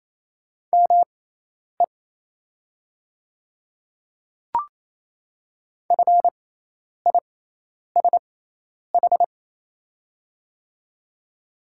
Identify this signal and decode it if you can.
Morse code, heard as a keyed tone, about 27 wpm, spaced as for 10 wpm, 700 Hz. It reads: ME FISH
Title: 6x Character Spacing - Top 300 Words - 28wpm